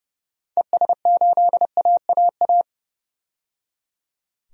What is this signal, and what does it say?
Morse code, heard as a keyed tone, about 30 wpm, 700 Hz. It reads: ES8AAA
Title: DXCC-100 Call Signs - 6 Characters - 30wpm